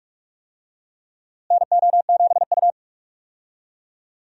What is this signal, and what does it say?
Morse code, heard as a keyed tone, about 46 wpm, 700 Hz. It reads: NO7U
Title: USA Call Signs - 4 Characters - Repeated 2x - 45wpm